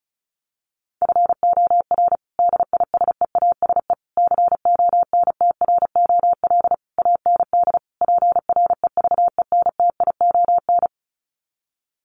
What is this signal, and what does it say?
Morse code, heard as a keyed tone, about 34 wpm, 700 Hz. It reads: FOR DISEASE CONTROL AND PREVENTION
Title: Sets of 5 Words Encore2 35wpm